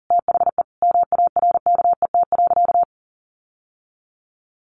Morse code, heard as a keyed tone, about 37 wpm, 700 Hz. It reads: THE MARKET.
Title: Sentences from Top 1400 Words 40wpm